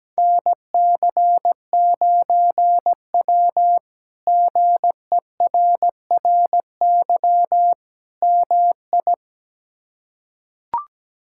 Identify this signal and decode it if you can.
Morse code, heard as a keyed tone, about 17 wpm, 700 Hz. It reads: NC9W GERRY MI
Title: K1USN SST QSO Practice - 17wpm